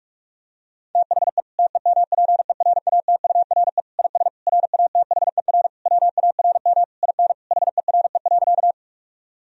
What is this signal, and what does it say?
Morse code, heard as a keyed tone, about 41 wpm, 700 Hz. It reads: THE TEMPERATURE IS RATHER WARM IN HERE.
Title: Sentences from Top 1000 Words - Review Part 4 - 45wpm